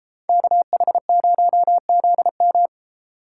Morse code, heard as a keyed tone, about 32 wpm, 700 Hz. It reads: KH0ZM